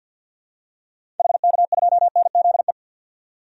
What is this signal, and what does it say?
Morse code, heard as a keyed tone, about 50 wpm, 700 Hz. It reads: HK2NZE
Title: DXCC-100 Call Signs - 6 Characters - 50wpm